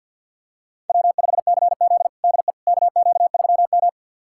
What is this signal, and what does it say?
Morse code, heard as a keyed tone, about 52 wpm, 700 Hz. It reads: W5CZ DE KQ3M
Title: USA Callsign DE Callsign - 4 Characters - 50wpm